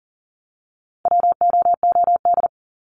Morse code, heard as a keyed tone, about 40 wpm, 700 Hz. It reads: WOOD
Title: Top 300 Words 40wpm